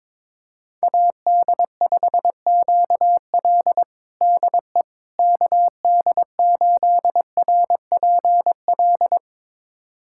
Morse code, heard as a keyed tone, about 21 wpm, 700 Hz. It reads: AD5QL DE KD8RPL
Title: USA Callsign DE Callsign 22wpm